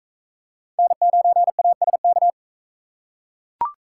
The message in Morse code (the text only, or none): N9ASK